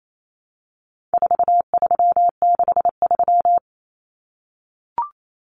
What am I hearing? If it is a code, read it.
Morse code, heard as a keyed tone, about 28 wpm, 700 Hz. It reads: 4363
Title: Common Cut Numbers - 4-Digit 28wpm